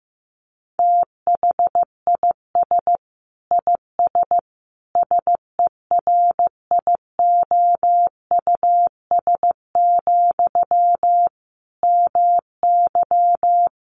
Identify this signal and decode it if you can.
Morse code, heard as a keyed tone, about 15 wpm, 700 Hz. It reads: THIS IS SERIOUS, MY
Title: Sentences from Top 1100 Words 15wpm